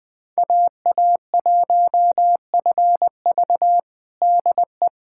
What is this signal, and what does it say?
Morse code, heard as a keyed tone, about 21 wpm, 700 Hz. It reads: AA1FV DE
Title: USA Callsign DE Callsign - 5 Characters - 20wpm